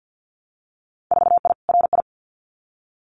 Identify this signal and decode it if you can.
Morse code, heard as a keyed tone, about 50 wpm, 700 Hz. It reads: 4I RI
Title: Winter Field Day Exchanges Encore 50wpm